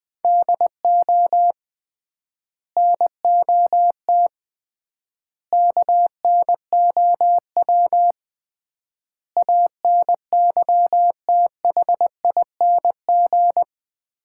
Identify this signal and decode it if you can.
Morse code, heard as a keyed tone, about 21 wpm, 700 Hz. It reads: DO NOT KNOW ANYTHING
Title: 3x Word-Spacing - Sets of 5 Words - 20wpm